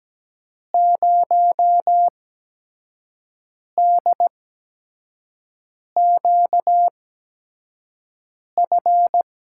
Morse code, heard as a keyed tone, about 17 wpm, spaced as for 5 wpm, 700 Hz. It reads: 0DQF
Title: USA Call Signs - 6 Characters - 8x Character Spacing - 17wpm